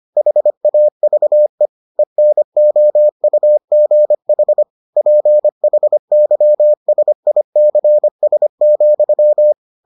Morse code, heard as a keyed tone, about 25 wpm, 600 Hz. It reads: HAVE ENOUGH PHYSICS,